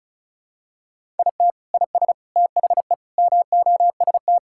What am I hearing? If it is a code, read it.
Morse code, heard as a keyed tone, about 36 wpm, 700 Hz. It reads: IT IS THE MOST
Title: Sentences from Top 1000 Words - Review Part 6 - 35wpm